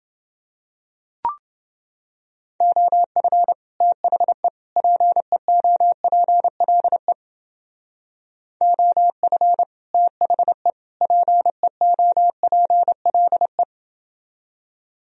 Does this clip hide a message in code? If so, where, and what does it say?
Yes: Morse code from 2.6 to 13.6 s, heard as a keyed tone, about 29 wpm, 700 Hz. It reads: OF THE PEOPLE OF THE PEOPLE